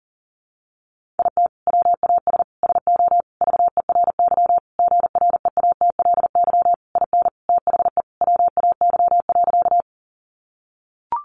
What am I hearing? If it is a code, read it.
Morse code, heard as a keyed tone, about 40 wpm, 700 Hz. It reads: IT WAS SO VERY GREATLY IN THE WAY.
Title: Sentences from Top 1400 Words 40wpm